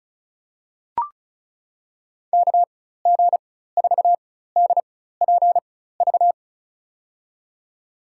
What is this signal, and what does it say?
Morse code, heard as a keyed tone, about 34 wpm, spaced as for 16 wpm, 700 Hz. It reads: KG4DPV
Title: USA Call Signs - 6 Characters - 4x Character Spacing - 35wpm